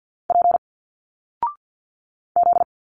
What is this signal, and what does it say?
Morse code, heard as a keyed tone, about 51 wpm, 700 Hz. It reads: P Z
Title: Single Letters 50wpm